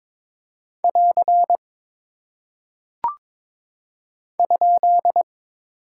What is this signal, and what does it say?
Morse code, heard as a keyed tone, about 22 wpm, 700 Hz. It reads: <AR> ?
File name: practice / 22wpm / Single Punctuation-Prosign - 22wpm